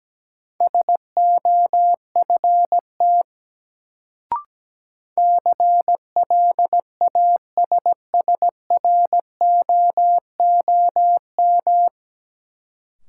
Morse code, heard as a keyed tone, about 17 wpm, 700 Hz. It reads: SOFT CLASSROOM
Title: Sentences from Top 1500 Words - New Word Drill - 17wpm